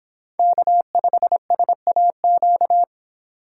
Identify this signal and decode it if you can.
Morse code, heard as a keyed tone, about 26 wpm, 700 Hz. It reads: K5SAQ